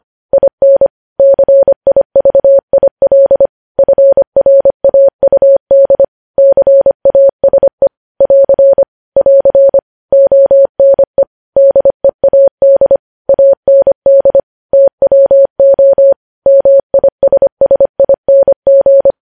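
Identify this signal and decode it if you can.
Morse code, heard as a keyed tone, about 25 wpm, 550 Hz. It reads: IN CIVIL FRAUD CASE <AR> <AR> ONE DEAD AND TWO MISSING
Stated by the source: News headlines sent in Morse at 25 wpm.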